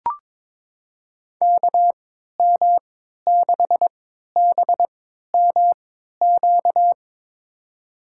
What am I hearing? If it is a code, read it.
Morse code, heard as a keyed tone, about 22 wpm, spaced as for 13 wpm, 700 Hz. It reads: KM6BMQ
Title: USA Call Signs - 6 Characters - 3x Character Spacing - 22wpm